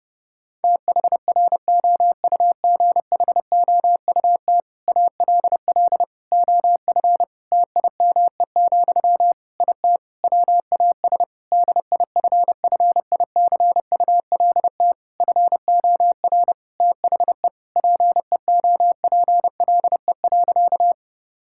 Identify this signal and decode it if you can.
Morse code, heard as a keyed tone, about 30 wpm, 700 Hz. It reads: THROUGHOUT ALL OF TIME, IT WAS DIFFICULT FOR THE PEOPLE.
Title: Sentences from Top 1000 Words - Review Part 5 - 30wpm